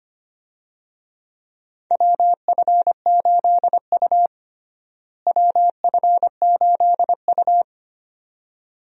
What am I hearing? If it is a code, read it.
Morse code, heard as a keyed tone, about 25 wpm, 700 Hz. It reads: WF8U WF8U
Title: USA Call Signs - 4 Characters - Repeated 2x - 25wpm